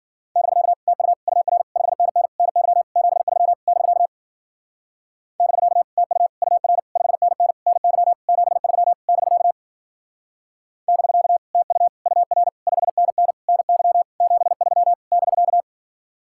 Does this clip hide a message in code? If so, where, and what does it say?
Yes: Morse code from 0.4 to 15.6 s, heard as a keyed tone, about 50 wpm, 700 Hz. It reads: <BK> TU UR 5NN NY 73 <BK> <BK> TU UR 5NN NY 73 <BK> <BK> TU UR 5NN NY 73 <BK>